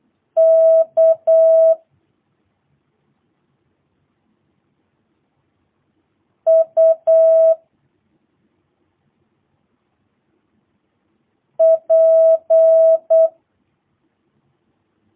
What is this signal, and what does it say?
Morse code, heard as a keyed tone, about 8 wpm, 650 Hz. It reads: K U P
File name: jh8cbh / 2022cw